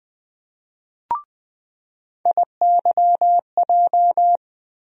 Morse code, heard as a keyed tone, about 20 wpm, 700 Hz. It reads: IYJ
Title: USA Call Sign Postfixes - 3 Letters 20wpm